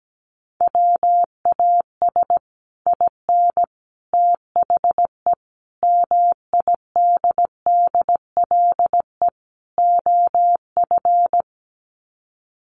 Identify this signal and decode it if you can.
Morse code, heard as a keyed tone, about 17 wpm, 700 Hz. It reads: WAS IN THE MIDDLE OF
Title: Sets of 5 Words Encore 17wpm